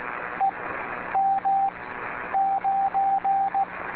Morse code, heard as a keyed tone, about 16 wpm, 780 Hz, with radio noise. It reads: E M 9
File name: beacons / w4clm-b